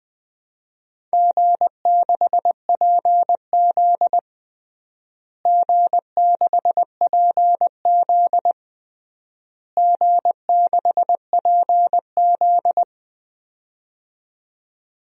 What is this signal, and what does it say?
Morse code, heard as a keyed tone, about 19 wpm, 700 Hz. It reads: G6PZ G6PZ G6PZ
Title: DXCC-100 Call Signs - 4 Characters - Repeated 3x - 20wpm